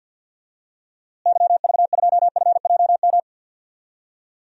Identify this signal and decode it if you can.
Morse code, heard as a keyed tone, about 53 wpm, 700 Hz. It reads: YV2FJM
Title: DXCC-100 Call Signs - 6 Characters - Repeated 2x - 50wpm